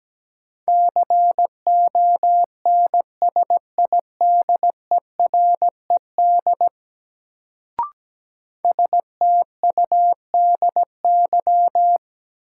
Morse code, heard as a keyed tone, about 17 wpm, 700 Hz. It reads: CONSIDERED STUDY
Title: Sets of 5 Words - New Word Drill - 17wpm